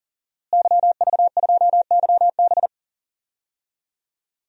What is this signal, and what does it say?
Morse code, heard as a keyed tone, about 40 wpm, 700 Hz. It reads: YV2YB
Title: DXCC-100 Call Signs - 5 Characters - Repeated 3x - 40wpm